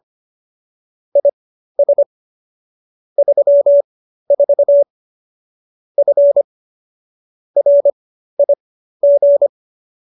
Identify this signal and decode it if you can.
Morse code, heard as a keyed tone, about 25 wpm, spaced as for 13 wpm, 570 Hz. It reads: IS 34 F RIG